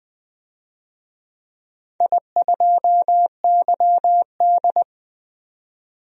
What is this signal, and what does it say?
Morse code, heard as a keyed tone, about 20 wpm, 700 Hz. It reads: I2YD
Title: DXCC-100 Call Signs - 4 Characters - 20wpm